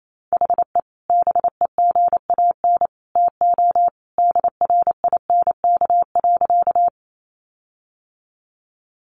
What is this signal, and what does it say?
Morse code, heard as a keyed tone, about 28 wpm, 700 Hz. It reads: HE BEGAN TO DRINK.
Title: Sentences from Top 1500 Words 28wpm